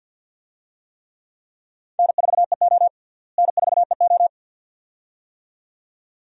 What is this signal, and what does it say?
Morse code, heard as a keyed tone, about 47 wpm, 700 Hz. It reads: N4EO N4EO
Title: USA Call Signs - 4 Characters - Repeated 2x - 50wpm